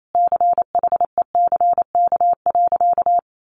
Morse code, heard as a keyed tone, about 28 wpm, 700 Hz. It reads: CHECK.
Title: A Sound Of Thunder 28wpm